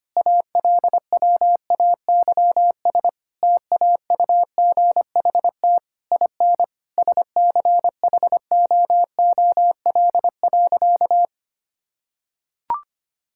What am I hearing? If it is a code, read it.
Morse code, heard as a keyed tone, about 25 wpm, 700 Hz. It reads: ALWAYS TAUGHT IN SCHOOL.
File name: practice / 25wpm / Difficult Sentences with Hints - 25wpm